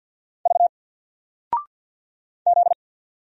Morse code, heard as a keyed tone, about 49 wpm, 700 Hz. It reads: V Z
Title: Single Character (Letter:Number:Punctuation:Prosign) 50wpm